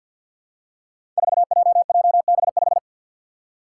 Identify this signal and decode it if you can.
Morse code, heard as a keyed tone, about 47 wpm, 700 Hz. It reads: 4JJB5
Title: Code Groups - 5 Letter:Numbers - 50wpm